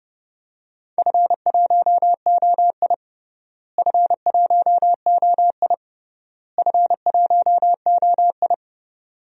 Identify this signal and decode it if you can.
Morse code, heard as a keyed tone, about 30 wpm, 700 Hz. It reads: F1OI F1OI F1OI
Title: DXCC-100 Call Signs - 4 Characters - Repeated 3x - 30wpm